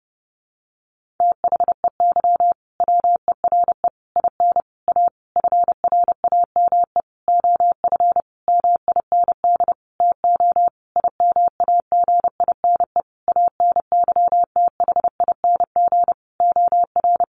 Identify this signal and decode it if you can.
Morse code, heard as a keyed tone, about 30 wpm, 700 Hz. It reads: THEY WERE IN A FRAME OF MIND TO IMAGINE ANYTHING OR
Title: Sentences from Top 1400 Words 30wpm